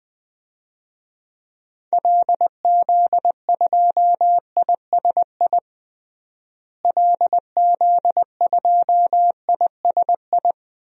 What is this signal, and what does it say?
Morse code, heard as a keyed tone, about 20 wpm, 700 Hz. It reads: LZ2ISI LZ2ISI
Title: DXCC-100 Call Signs - 6 Characters - Repeated 3x - 20wpm